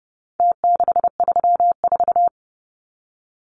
Morse code, heard as a keyed tone, about 30 wpm, 700 Hz. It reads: T634
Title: Numbers - Contest Style - 4-Digit 30wpm